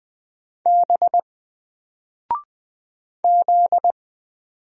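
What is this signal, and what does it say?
Morse code, heard as a keyed tone, about 20 wpm, 700 Hz. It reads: B Z
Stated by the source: Single letter and number Morse drill at 20 wpm, from N4LCD.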